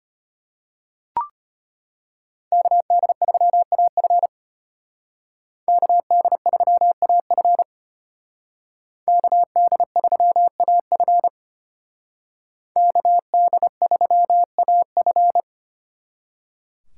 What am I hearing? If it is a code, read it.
Morse code, heard as a keyed tone, about 30 wpm, 700 Hz. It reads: KD3AF KD3AF KD3AF KD3AF
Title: Speed Racing - USA 2x2 Call Signs 25wpm